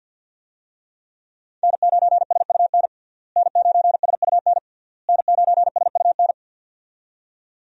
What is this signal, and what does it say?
Morse code, heard as a keyed tone, about 46 wpm, 700 Hz. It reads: N9SUN N9SUN N9SUN
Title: USA Call Signs - 5 Characters - Repeated 3x - 50wpm